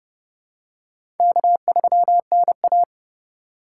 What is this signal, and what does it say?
Morse code, heard as a keyed tone, about 30 wpm, 700 Hz. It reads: K3NA
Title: USA Call Signs - 6x Character Spacing - 30wpm